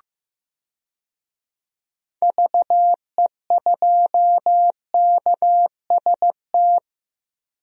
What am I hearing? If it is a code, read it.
Morse code, heard as a keyed tone, about 15 wpm, 700 Hz. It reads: VE2KST
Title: DXCC-100 Call Signs - 6 Characters - 15wpm